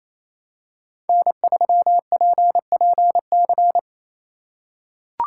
Morse code, heard as a keyed tone, about 28 wpm, 700 Hz. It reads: N3PPC
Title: USA Call Signs - 5 Characters - 4x Character Spacing - 28wpm